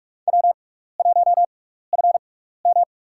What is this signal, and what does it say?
Morse code, heard as a keyed tone, about 41 wpm, spaced as for 15 wpm, 700 Hz. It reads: W1FM
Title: USA Call Signs - 4 Characters - 6x Character Spacing - 45wpm